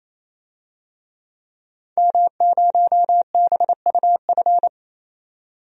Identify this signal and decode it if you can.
Morse code, heard as a keyed tone, about 28 wpm, 700 Hz. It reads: M0BUF
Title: DXCC-100 Call Signs - 5 Characters - 28wpm